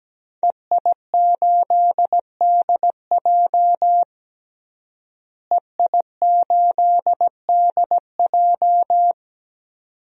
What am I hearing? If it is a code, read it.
Morse code, heard as a keyed tone, about 17 wpm, 700 Hz. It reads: EI8DJ EI8DJ
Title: DXCC-100 Call Signs - 5 Characters - Repeated 2x - 17wpm